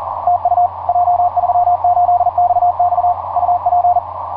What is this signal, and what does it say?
Morse code, heard as a keyed tone, about 39 wpm, 710 Hz. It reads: TU J38XX UP
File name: j38xx mw0cri etc 15m cw 18feb2010 1532